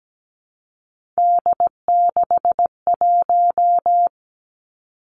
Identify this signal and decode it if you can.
Morse code, heard as a keyed tone, about 17 wpm, 700 Hz. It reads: D61
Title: Three Challenging Letter-Number Combinations 17wpm